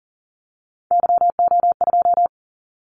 Morse code, heard as a keyed tone, about 40 wpm, 700 Hz. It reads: YO2
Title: Callsign Prefixes for Top 25 DXCC - 40wpm